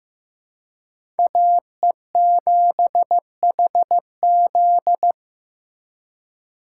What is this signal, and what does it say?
Morse code, heard as a keyed tone, about 15 wpm, 700 Hz. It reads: AE7HZ
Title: USA Call Signs - 5 Characters - 3x Character Spacing - 15wpm